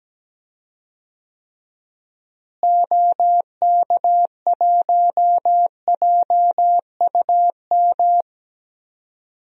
Morse code, heard as a keyed tone, about 17 wpm, 700 Hz. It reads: OK1JUM
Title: DXCC-100 Call Signs - 6 Characters - Repeated 2x - 17wpm